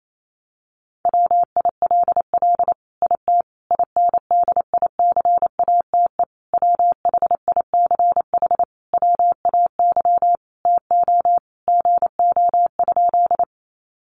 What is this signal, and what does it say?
Morse code, heard as a keyed tone, about 28 wpm, 700 Hz. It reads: WILL IT INDICATE WHICH WAY TO GO?
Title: Sentences from Top 1300 Words 28wpm